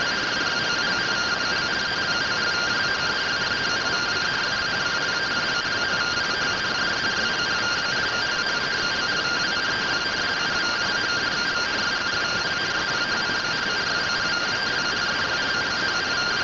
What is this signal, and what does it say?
Two-tone teleprinter signal, about 170 Hz shift.